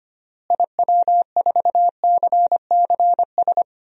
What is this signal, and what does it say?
Morse code, heard as a keyed tone, about 25 wpm, 700 Hz. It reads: IW4CCS
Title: DXCC-100 Call Signs - 6 Characters - Repeated 3x - 25wpm